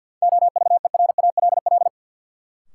Morse code, heard as a keyed tone, about 49 wpm, 700 Hz. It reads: OVERALL
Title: Sentences from Top 1000 Words - Review Part 7 - New Word Drill - 50wpm